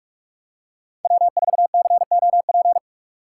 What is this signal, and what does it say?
Morse code, heard as a keyed tone, about 44 wpm, 700 Hz. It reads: W4COP
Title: USA Call Signs - 5 Characters - 3x Character Spacing - 45wpm